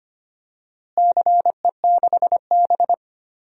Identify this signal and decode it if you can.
Morse code, heard as a keyed tone, about 25 wpm, 700 Hz. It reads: CE6B
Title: DXCC-100 Call Signs - 4 Characters - Repeated 2x - 25wpm